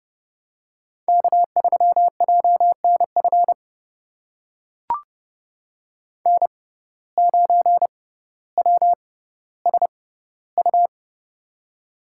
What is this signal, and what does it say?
Morse code, heard as a keyed tone, about 32 wpm, spaced as for 16 wpm, 700 Hz. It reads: K3JNF N9WSU
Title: USA Call Signs - 5 Characters - 6x Character Spacing - 30wpm